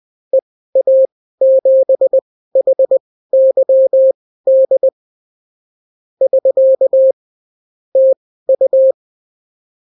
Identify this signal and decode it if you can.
Morse code, heard as a keyed tone, about 20 wpm, spaced as for 14 wpm, 530 Hz. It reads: EA7HYD <SK> TU
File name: CW QSO libroCW4 20-10ppm